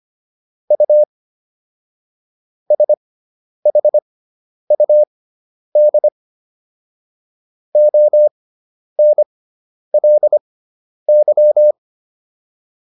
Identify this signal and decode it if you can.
Morse code, heard as a keyed tone, about 25 wpm, spaced as for 10 wpm, 600 Hz. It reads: U SHUD ONLY